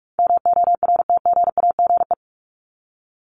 Morse code, heard as a keyed tone, about 44 wpm, 700 Hz. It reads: MORTGAGE
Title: Top 1751-2000 Words 45wpm